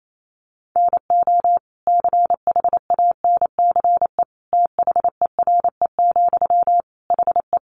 Morse code, heard as a keyed tone, about 28 wpm, 700 Hz. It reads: NO CHANCE THERE, HE
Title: Sentences from Top 800 Words 28wpm